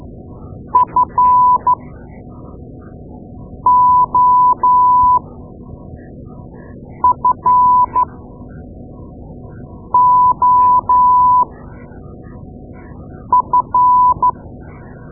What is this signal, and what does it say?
Morse code, heard as a keyed tone, about 12 wpm, spaced as for 4 wpm, 1000 Hz. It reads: FOFOF
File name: Morse Lessons / Morse Lessons / Tape-2A